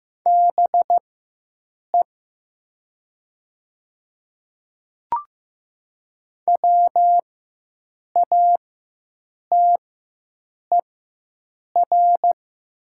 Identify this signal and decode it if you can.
Morse code, heard as a keyed tone, about 15 wpm, spaced as for 7 wpm, 700 Hz. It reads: BE WATER
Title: 4x Character Spacing - Top 200 Words - 15wpm